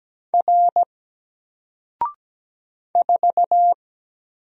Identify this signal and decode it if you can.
Morse code, heard as a keyed tone, about 17 wpm, 700 Hz. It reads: R 4
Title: Single Letter-Number 17wpm